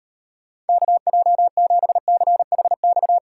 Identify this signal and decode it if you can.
Morse code, heard as a keyed tone, about 38 wpm, 700 Hz. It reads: KJ7CHX